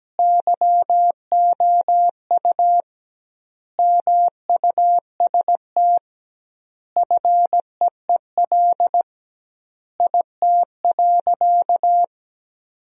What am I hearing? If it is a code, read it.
Morse code, heard as a keyed tone, about 17 wpm, 700 Hz. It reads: YOU MUST FEEL IT.
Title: 2x Word-Spacing - Sentences from Top 400 Words - 17wpm